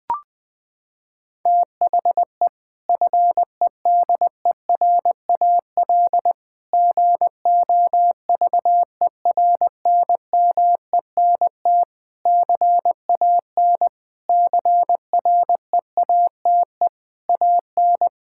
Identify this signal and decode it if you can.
Morse code, heard as a keyed tone, about 21 wpm, 700 Hz. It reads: THE FEDERAL GOVERNMENT CAN CREATE AN
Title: Sentences from Top 1600 Words 20wpm